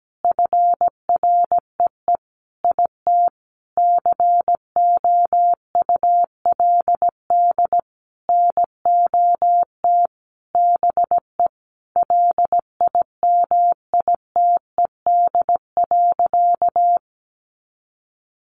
Morse code, heard as a keyed tone, about 17 wpm, 700 Hz. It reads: FREE IT COULD NOT BE LIMITED.
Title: Sentences from Top 900 Words 17wpm